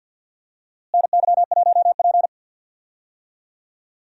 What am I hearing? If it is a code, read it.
Morse code, heard as a keyed tone, about 46 wpm, 700 Hz. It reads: NY1P
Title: USA Call Signs - 4 Characters - Repeated 2x - 50wpm